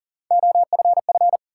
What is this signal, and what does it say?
Morse code, heard as a keyed tone, about 40 wpm, 700 Hz. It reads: OFF
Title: Sets of 2 Words - New Word Drill - 40wpm